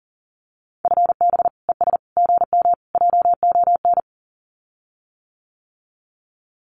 Morse code, heard as a keyed tone, about 40 wpm, 700 Hz. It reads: FB ES GM JON
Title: Common QSO Phrases Encore 40wpm